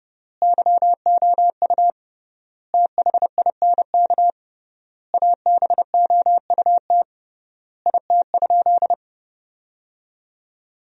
Morse code, heard as a keyed tone, about 32 wpm, 700 Hz. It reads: YOU THINK ABOUT IT?
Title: 3x Word-Spacing - Sentences from Top 200 Words - 30wpm